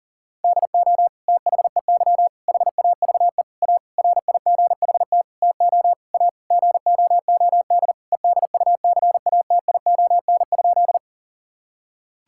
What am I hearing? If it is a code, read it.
Morse code, heard as a keyed tone, about 40 wpm, 700 Hz. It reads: DO THEY HAVE A RIGHT TO A GOOD EDUCATION?
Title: Sentences from Top 1000 Words - Review Part 6 - 40wpm